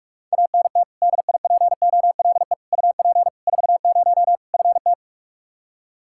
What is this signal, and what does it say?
Morse code, heard as a keyed tone, about 47 wpm, 700 Hz. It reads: ANT DIPOLE UP 40 FT
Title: Practice QSO - Exchange 2 - Info - 45wpm